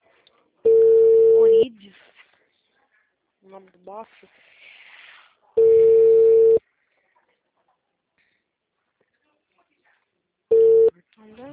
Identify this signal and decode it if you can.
Ringback tone (China, 450 Hz), with speech.